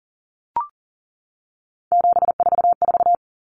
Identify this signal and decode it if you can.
Morse code, heard as a keyed tone, about 40 wpm, 700 Hz. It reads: 744
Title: Common Cut Numbers - 3-Digit 40wpm